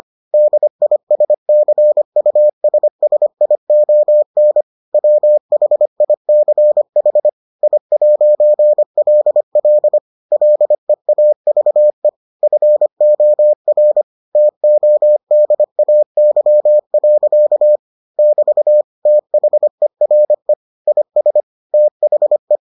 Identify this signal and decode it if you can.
Morse code, heard as a keyed tone, about 25 wpm, 600 Hz. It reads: DISCUSSION WHICH I'LL LEAVE FOR TODAY. <BT> THERE IS THE